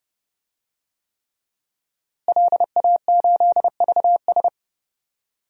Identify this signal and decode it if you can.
Morse code, heard as a keyed tone, about 30 wpm, 700 Hz. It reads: LA8VS
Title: DXCC-100 Call Signs - 5 Characters - Repeated 3x - 30wpm